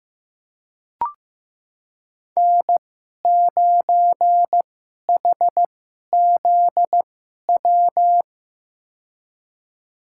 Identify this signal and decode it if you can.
Morse code, heard as a keyed tone, about 15 wpm, spaced as for 11 wpm, 700 Hz. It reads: N9HZW
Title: USA Call Signs - 5 Characters - 2x Character Spacing - 15wpm